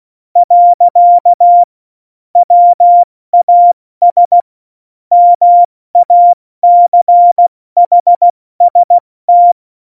Morse code, heard as a keyed tone, about 16 wpm, 700 Hz. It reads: . WAS MACHST
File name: Morsekurs / Witze / MP3, Witz 24 April 2018, 70 F80